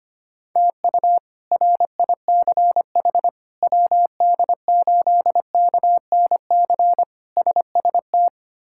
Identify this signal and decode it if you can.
Morse code, heard as a keyed tone, about 25 wpm, 700 Hz. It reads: TU RICH WD8KNC SST
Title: K1USN SST QSO Practice - 25wpm